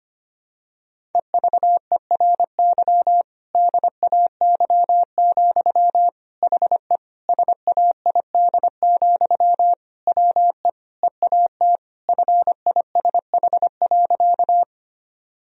Morse code, heard as a keyed tone, about 25 wpm, 700 Hz. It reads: EVERY DAY, HE SAID, WE EAT FISH.
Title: Sentences from Top 1000 Words - Review Part 1 - 25wpm